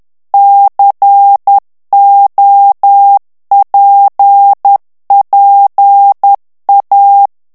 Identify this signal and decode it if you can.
Morse code, heard as a keyed tone, about 10 wpm, 800 Hz. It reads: COPPA
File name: ITALIEN texte clair 2 vitesse 10 mots minute